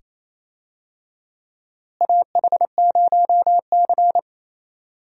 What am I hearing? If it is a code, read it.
Morse code, heard as a keyed tone, about 28 wpm, 700 Hz. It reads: AH0C
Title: USA Call Signs - 4 Characters - Repeated 2x - 28wpm